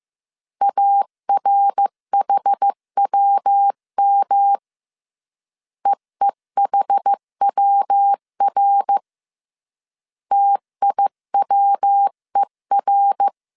Morse code, heard as a keyed tone, about 15 wpm, 800 Hz. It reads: ARHWM EEHWR TIWER
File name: CW / DET / 1 3 1500C S A 14wpm 2447